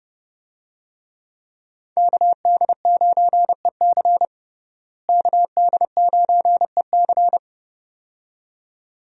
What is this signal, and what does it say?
Morse code, heard as a keyed tone, about 30 wpm, 700 Hz. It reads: KD9EC KD9EC
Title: USA Call Signs - 5 Characters - Repeated 2x - 30wpm